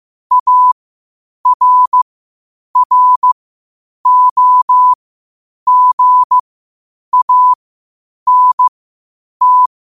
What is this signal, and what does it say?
Morse code, heard as a keyed tone, about 15 wpm, spaced as for 8 wpm, 1000 Hz. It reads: ARROGANT
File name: qotdfiles / 8wpm